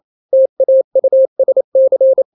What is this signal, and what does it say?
Morse code, heard as a keyed tone, about 28 wpm, 530 Hz. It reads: TAUSC
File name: daily / 27wpm